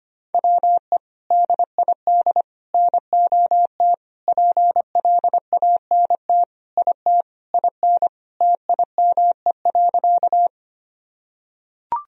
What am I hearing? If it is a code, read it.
Morse code, heard as a keyed tone, about 25 wpm, 700 Hz. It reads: WE DID NOT PLANT IT IN TIME.
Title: Sentences from Top 1000 Words - Review Part 1 - 25wpm